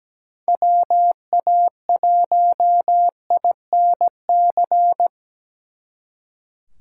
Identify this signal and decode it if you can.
Morse code, heard as a keyed tone, about 19 wpm, 700 Hz. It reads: WA1INC
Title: Speed Racing - USA 2x3 Call Signs 17wpm